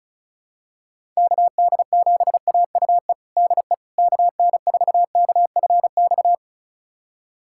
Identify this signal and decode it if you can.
Morse code, heard as a keyed tone, about 34 wpm, 690 Hz. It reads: KD7AUE DE KN4KFX
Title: USA Callsign DE Callsign - 6 Characters - 35wpm